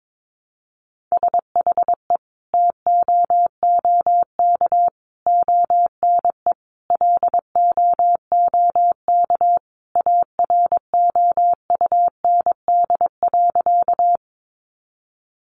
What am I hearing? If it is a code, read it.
Morse code, heard as a keyed tone, about 22 wpm, 700 Hz. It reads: SHE TOOK ONE LOOK AROUND.
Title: Sentences from Top 600 Words 22wpm